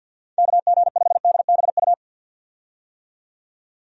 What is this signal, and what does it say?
Morse code, heard as a keyed tone, about 48 wpm, 700 Hz. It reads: KG5DBU
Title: USA Call Signs - 6 Characters - 50wpm